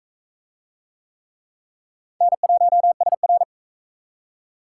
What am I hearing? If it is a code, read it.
Morse code, heard as a keyed tone, about 34 wpm, 700 Hz. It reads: N1SR
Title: Speed Racing - USA 1x2 Call Signs 28wpm